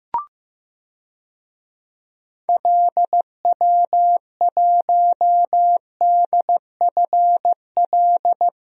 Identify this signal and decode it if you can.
Morse code, heard as a keyed tone, about 15 wpm, 700 Hz. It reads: LW1DFL
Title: DXCC-100 Call Signs - 6 Characters - Repeated 3x - 15wpm